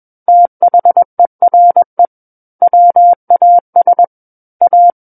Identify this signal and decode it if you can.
Morse code, heard as a keyed tone, about 21 wpm, 700 Hz. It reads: THERE WAS A
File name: Book One / 12-Chapter 12 - 21 WPM